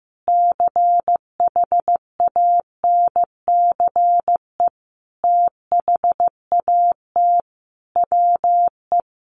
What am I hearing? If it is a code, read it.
Morse code, heard as a keyed tone, about 15 wpm, 700 Hz. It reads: CHANCE THAT WE
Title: Sentences from Top 800 Words 15wpm